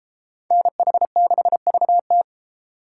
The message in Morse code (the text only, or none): NH6VT